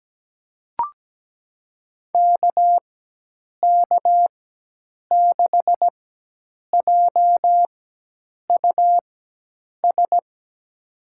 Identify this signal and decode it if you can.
Morse code, heard as a keyed tone, about 17 wpm, spaced as for 8 wpm, 700 Hz. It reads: KK6JUS
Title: USA Call Signs - 6 Characters - 4x Character Spacing - 17wpm